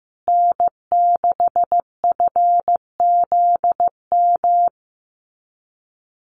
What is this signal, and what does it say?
Morse code, heard as a keyed tone, about 15 wpm, 700 Hz. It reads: N6FZM
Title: USA Call Signs Repeated 2x 15wpm